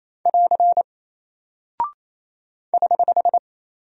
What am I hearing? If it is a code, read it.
Morse code, heard as a keyed tone, about 28 wpm, 700 Hz. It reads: <AR> <HH>
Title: Single Character (Letter:Number:Punctuation:Prosign) Alt 28wpm